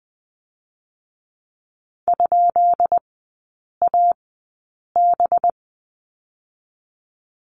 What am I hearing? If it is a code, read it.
Morse code, heard as a keyed tone, about 20 wpm, spaced as for 8 wpm, 700 Hz. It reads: ?AB
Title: 184 Lesson 30e - Letter b - Three Characters 20wpm